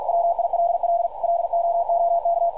Morse code, heard as a keyed tone, about 52 wpm, 680 Hz. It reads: WILL LOOK
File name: archived / CRSnet